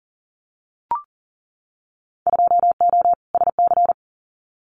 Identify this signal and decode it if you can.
Morse code, heard as a keyed tone, about 41 wpm, 700 Hz. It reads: 2O SC
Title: Winter Field Day Exchanges Encore 40wpm